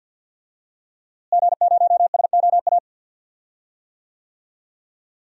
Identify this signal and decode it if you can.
Morse code, heard as a keyed tone, about 46 wpm, 700 Hz. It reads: G0SOA
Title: DXCC-100 Call Signs - 5 Characters - Repeated 3x - 50wpm